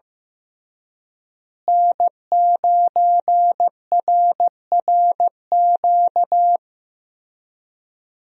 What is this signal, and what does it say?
Morse code, heard as a keyed tone, about 15 wpm, 700 Hz. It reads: N9RRQ
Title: USA Call Signs - 5 Characters - Repeated 2x - 15wpm